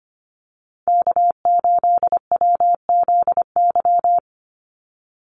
Morse code, heard as a keyed tone, about 25 wpm, 700 Hz. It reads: K8WZY
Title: USA Call Signs 25wpm